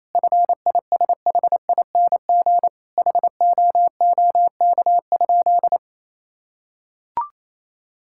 Morse code, heard as a keyed tone, about 28 wpm, 700 Hz. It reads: FISHING HOOK?
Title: Sentences from Top 1600 Words 28wpm